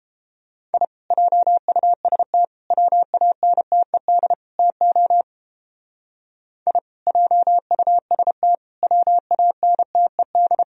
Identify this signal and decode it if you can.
Morse code, heard as a keyed tone, about 29 wpm, 700 Hz. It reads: I JUST WANTED TO I JUST WANTED